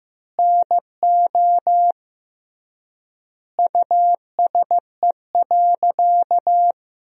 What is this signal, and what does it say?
Morse code, heard as a keyed tone, about 16 wpm, 700 Hz. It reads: NO USE.
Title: 3x Word-Spacing - Sentences from Top 100 Words - 15wpm